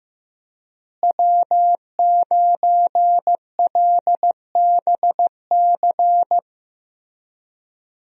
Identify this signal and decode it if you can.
Morse code, heard as a keyed tone, about 15 wpm, 700 Hz. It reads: W9LBC
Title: USA Call Signs - 5 Characters - Repeated 3x - 15wpm